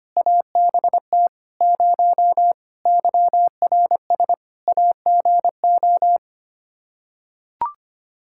Morse code, heard as a keyed tone, about 25 wpm, 700 Hz. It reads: ABT 0 YRS AGO
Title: K1USN SST QSO Practice 25wpm